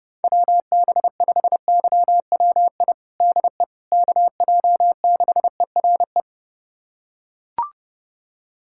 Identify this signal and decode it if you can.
Morse code, heard as a keyed tone, about 30 wpm, 700 Hz. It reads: WB5YWI DE KJ6ERE
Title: USA Callsign DE Callsign - 6 Characters - 30wpm